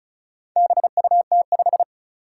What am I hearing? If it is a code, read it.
Morse code, heard as a keyed tone, about 36 wpm, 700 Hz. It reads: BUT5
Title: Four Challenging Letter-Number Combinations 35wpm